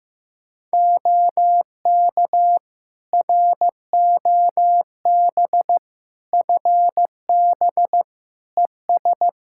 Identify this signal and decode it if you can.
Morse code, heard as a keyed tone, about 16 wpm, 700 Hz. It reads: OK ROB FB ES
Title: Practice QSO - Exchange 2 - Info - 15wpm